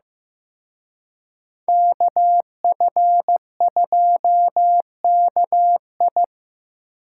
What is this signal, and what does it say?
Morse code, heard as a keyed tone, about 18 wpm, 700 Hz. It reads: KF2KI
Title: Speed Racing - USA 2x2 Call Signs 15wpm